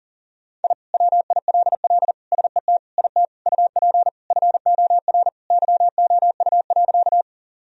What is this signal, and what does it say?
Morse code, heard as a keyed tone, about 40 wpm, 700 Hz. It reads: I WILL SET IT UP FOR YOU.
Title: Sentences from Top 1000 Words - Review Part 1 - 40wpm